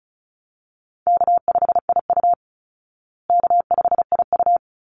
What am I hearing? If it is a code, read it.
Morse code, heard as a keyed tone, about 34 wpm, 700 Hz. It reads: K5IU K5IU
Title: Easy Call Signs Repeated 2x - 35wpm